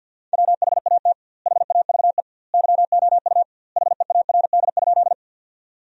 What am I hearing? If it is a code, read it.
Morse code, heard as a keyed tone, about 50 wpm, 700 Hz. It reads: WHAT HAVE YOU HEARD?
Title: 2x Word-Spacing - Sentences from Top 400 Words - 50wpm